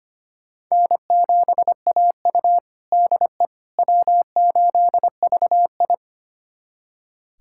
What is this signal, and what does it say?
Morse code, heard as a keyed tone, about 25 wpm, 700 Hz. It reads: N7AU DE W8VI
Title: USA Callsign DE Callsign - 4 Characters - 25wpm